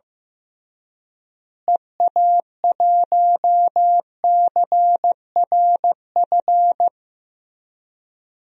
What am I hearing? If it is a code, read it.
Morse code, heard as a keyed tone, about 15 wpm, 700 Hz. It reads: EA1CRF
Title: DXCC-100 Call Signs - 6 Characters - Repeated 2x - 15wpm